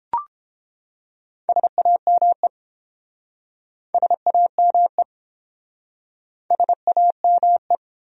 Morse code, heard as a keyed tone, about 29 wpm, 700 Hz. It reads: SAME SAME SAME